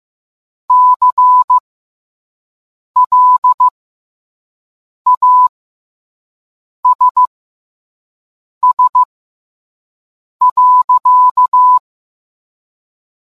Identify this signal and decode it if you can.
Morse code, heard as a keyed tone, about 15 wpm, spaced as for 5 wpm, 1000 Hz. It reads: CLASS.